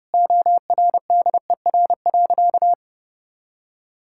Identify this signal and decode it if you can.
Morse code, heard as a keyed tone, about 30 wpm, 700 Hz. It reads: ORDER.
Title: Sentences from Top 1000 Words - Review Part 6 - 30wpm